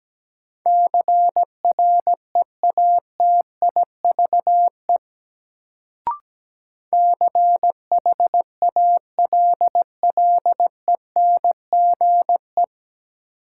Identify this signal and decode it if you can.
Morse code, heard as a keyed tone, about 17 wpm, 700 Hz. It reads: CREATIVE CHALLENGE
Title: Sentences from Top 1300 Words - New Word Drill Alt - 17wpm